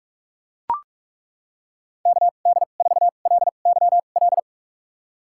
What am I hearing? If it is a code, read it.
Morse code, heard as a keyed tone, about 42 wpm, spaced as for 31 wpm, 700 Hz. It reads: KD4LYL
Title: USA Call Signs - 6 Characters - 2x Character Spacing - 45wpm